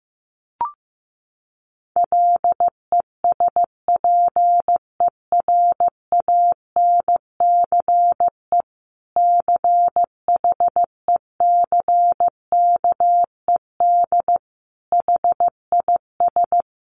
Morse code, heard as a keyed tone, about 15 wpm, 700 Hz. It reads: LESPERANCE CHECKED HIS
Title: A Sound Of Thunder 15wpm